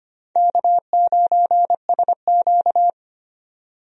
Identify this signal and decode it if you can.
Morse code, heard as a keyed tone, about 25 wpm, 700 Hz. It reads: K9SQ
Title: USA Call Signs - 4 Characters - 25wpm